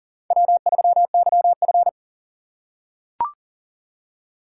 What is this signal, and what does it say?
Morse code, heard as a keyed tone, about 40 wpm, 700 Hz. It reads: W3YF
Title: USA Call Signs - 4 Characters - 3x Character Spacing - 40wpm